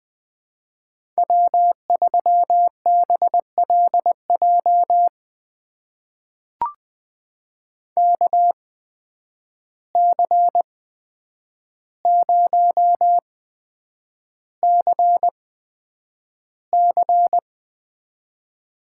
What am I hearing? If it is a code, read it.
Morse code, heard as a keyed tone, about 21 wpm, spaced as for 9 wpm, 700 Hz. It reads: W3BLJ KC0CC
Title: USA Call Signs - 5 Characters - 8x Character Spacing - 20wpm